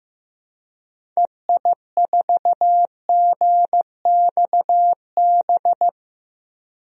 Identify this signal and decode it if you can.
Morse code, heard as a keyed tone, about 15 wpm, 700 Hz. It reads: EI4GXB
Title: DXCC-100 Call Signs - 6 Characters - 15wpm